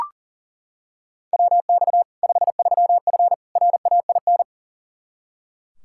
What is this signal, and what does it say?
Morse code, heard as a keyed tone, about 40 wpm, 700 Hz. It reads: WX 53F RAIN
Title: Practice QSO - Exchange 2 - Info - 40wpm